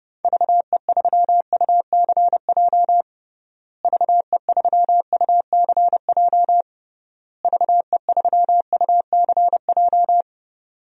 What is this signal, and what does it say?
Morse code, heard as a keyed tone, about 30 wpm, 700 Hz. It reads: VE3UCJ VE3UCJ VE3UCJ
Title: DXCC-100 Call Signs - 6 Characters - Repeated 3x - 30wpm